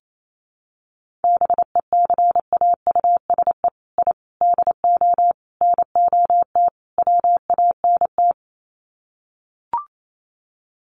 Morse code, heard as a keyed tone, about 28 wpm, 700 Hz. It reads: BECAUSE I DO NOT WANT
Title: Sets of 5 Words Encore 28wpm